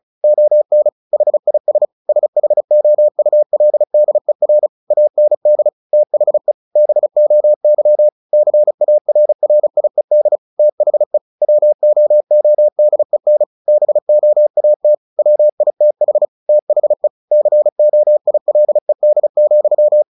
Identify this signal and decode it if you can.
Morse code, heard as a keyed tone, about 34 wpm, 600 Hz. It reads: ON HIS SHOULDER AND THE BOY CARRIED THE WOODEN BOAT WITH THE COILED,
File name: the old man and the sea / the old man and the sea 1 35wpm